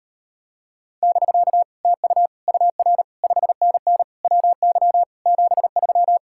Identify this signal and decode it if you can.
Morse code, heard as a keyed tone, about 33 wpm, 700 Hz. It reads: <BK> TU UR 5NN WY 73